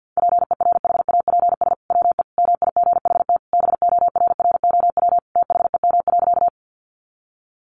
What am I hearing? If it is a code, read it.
Morse code, heard as a keyed tone, about 50 wpm, 700 Hz. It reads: PERHAPS WE MIGHT BORROW THEM.
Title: Sentences from Top 1500 Words 50wpm